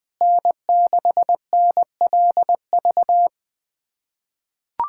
Morse code, heard as a keyed tone, about 21 wpm, 700 Hz. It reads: N6NLV
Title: USA Callsign DE Callsign 20wpm